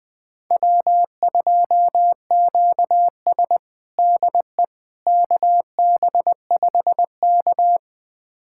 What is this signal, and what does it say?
Morse code, heard as a keyed tone, about 20 wpm, 700 Hz. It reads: W2QS DE KB5K
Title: USA Callsign DE Callsign - 4 Characters - 20wpm